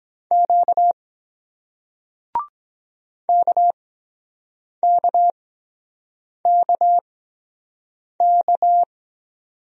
Morse code, heard as a keyed tone, about 23 wpm, spaced as for 7 wpm, 700 Hz. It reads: Q KKKK